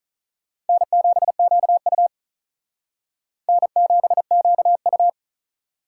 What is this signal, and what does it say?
Morse code, heard as a keyed tone, about 43 wpm, 700 Hz. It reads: N7QU N7QU
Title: Speed Racing - USA 1x2 Call Signs 35wpm